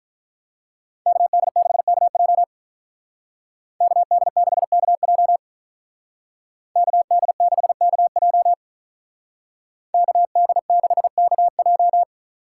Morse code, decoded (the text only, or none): KD6KJ KD6KJ KD6KJ KD6KJ